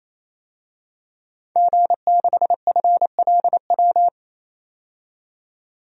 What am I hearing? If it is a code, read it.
Morse code, heard as a keyed tone, about 28 wpm, 700 Hz. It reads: G6FLW
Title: DXCC-100 Call Signs - 5 Characters - 28wpm